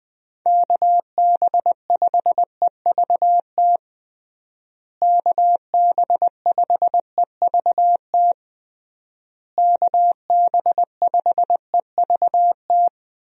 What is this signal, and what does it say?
Morse code, heard as a keyed tone, about 20 wpm, 700 Hz. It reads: KB5EVT KB5EVT KB5EVT
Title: USA Call Signs - 6 Characters - Repeated 3x - 20wpm